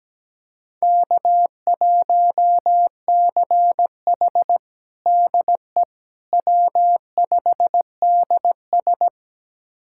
Morse code, heard as a keyed tone, about 17 wpm, 700 Hz. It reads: K1CH DE W5DS
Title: USA Callsign DE Callsign - 4 Characters - 17wpm